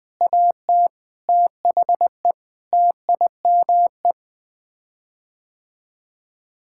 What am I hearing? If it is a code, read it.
Morse code, heard as a keyed tone, about 20 wpm, 700 Hz. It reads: AT THE TIME
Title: Sets of 8 Words - 20wpm